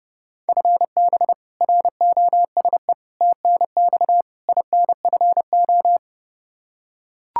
Morse code, heard as a keyed tone, about 30 wpm, 700 Hz. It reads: FB ROSE TNX INFO
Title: Practice QSO - Exchange 3 - More Info - 30wpm